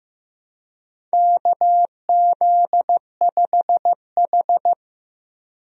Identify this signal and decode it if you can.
Morse code, heard as a keyed tone, about 15 wpm, 700 Hz. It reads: KZ5H
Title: USA Call Signs - 4 Characters - 15wpm